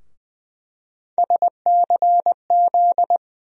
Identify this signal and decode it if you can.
Morse code, heard as a keyed tone, about 20 wpm, 700 Hz. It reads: SCZ
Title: USA Call Sign Postfixes - 3 Letters 20wpm